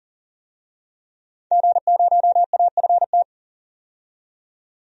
Morse code, heard as a keyed tone, about 40 wpm, 700 Hz. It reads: G0AFT
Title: DXCC-100 Call Signs - 5 Characters - 40wpm